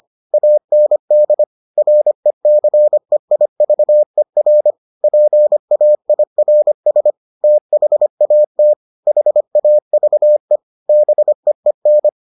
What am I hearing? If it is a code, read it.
Morse code, heard as a keyed tone, about 25 wpm, 600 Hz. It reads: AND RECEIVER PAIRS THAT HAVE BEEN